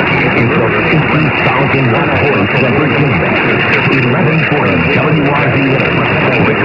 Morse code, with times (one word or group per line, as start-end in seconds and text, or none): none